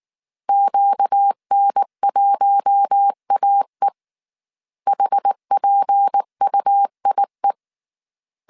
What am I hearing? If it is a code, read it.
Morse code, heard as a keyed tone, about 19 wpm, 800 Hz. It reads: QN1AE HPUIE